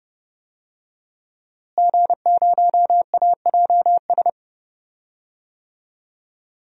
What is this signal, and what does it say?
Morse code, heard as a keyed tone, about 32 wpm, 700 Hz. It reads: G0AJS
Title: DXCC-100 Call Signs - 5 Characters - Repeated 3x - 30wpm